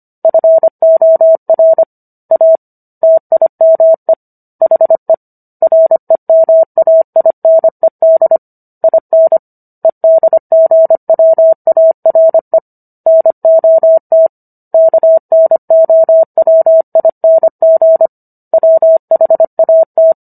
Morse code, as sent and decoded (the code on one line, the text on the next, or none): ..-. --- .-. / .- / - .. -- . / .... / .-. -- .- .. -. -.. / .. -. / . -.. --. .-- .- .-. / -. --- - / -.- -. --- .-- .. -. --. / .-- .... .- -
FOR A TIME HE REMAINED IN EDGWARE NOT KNOWING WHAT